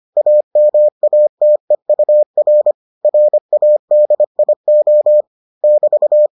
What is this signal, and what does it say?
Morse code, heard as a keyed tone, about 25 wpm, 600 Hz. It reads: AMATEUR RADIO <BT>